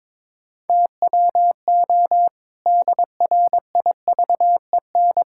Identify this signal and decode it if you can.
Morse code, heard as a keyed tone, about 22 wpm, 700 Hz. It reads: TWO DRIVEN
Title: Pangram Challenge v2 - 22wpm